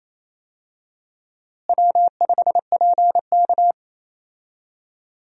Morse code, heard as a keyed tone, about 28 wpm, 700 Hz. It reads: W5PK
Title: USA Call Signs - 4 Characters - Repeated 3x - 28wpm